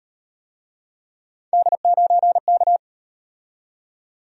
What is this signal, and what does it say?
Morse code, heard as a keyed tone, about 38 wpm, 700 Hz. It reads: D9K